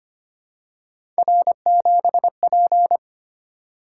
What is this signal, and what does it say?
Morse code, heard as a keyed tone, about 25 wpm, 700 Hz. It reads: R7P
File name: Three Letter-Number Combinations - 25wpm